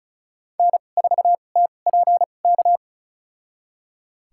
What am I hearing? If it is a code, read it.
Morse code, heard as a keyed tone, about 36 wpm, spaced as for 25 wpm, 700 Hz. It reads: N4TPK
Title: USA Call Signs - 2x Character Spacing - 35wpm